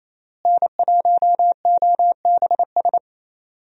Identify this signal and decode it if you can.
Morse code, heard as a keyed tone, about 28 wpm, 700 Hz. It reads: N1OBS
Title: USA Call Signs - 5 Characters - Repeated 3x - 28wpm